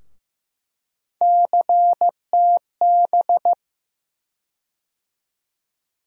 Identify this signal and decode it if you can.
Morse code, heard as a keyed tone, about 15 wpm, 700 Hz. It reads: CTB
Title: USA Call Sign Postfixes - 3 Letters 15wpm